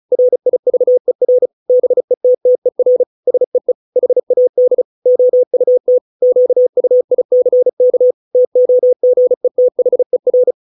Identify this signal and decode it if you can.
Morse code, heard as a keyed tone, about 34 wpm, 500 Hz. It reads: RIVER BETTER SEE HAD OUT QUICK TOGETHER